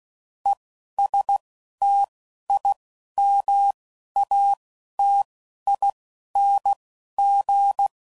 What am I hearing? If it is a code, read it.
Morse code, heard as a keyed tone, about 16 wpm, 800 Hz. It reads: E S T I M A T I N G